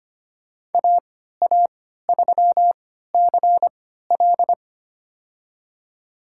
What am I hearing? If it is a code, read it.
Morse code, heard as a keyed tone, about 25 wpm, spaced as for 14 wpm, 700 Hz. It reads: AA3CL
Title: USA Call Signs - 3x Character Spacing - 25wpm